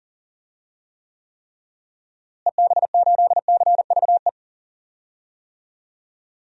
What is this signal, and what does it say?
Morse code, heard as a keyed tone, about 38 wpm, 700 Hz. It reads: EB8CVE
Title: DXCC-100 Call Signs - 6 Characters - Repeated 3x - 40wpm